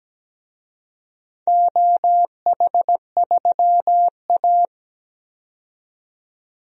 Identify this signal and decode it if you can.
Morse code, heard as a keyed tone, about 17 wpm, 700 Hz. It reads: OH3A
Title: DXCC-100 Call Signs - 4 Characters - 17wpm